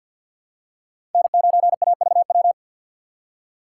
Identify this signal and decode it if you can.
Morse code, heard as a keyed tone, about 52 wpm, 700 Hz. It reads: N9AVW
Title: USA Call Signs - 5 Characters - 4x Character Spacing - 50wpm